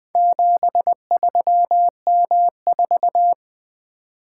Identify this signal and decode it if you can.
Morse code, heard as a keyed tone, about 20 wpm, 700 Hz. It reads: 73M4
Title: Four Letter-Number Combinations - 20wpm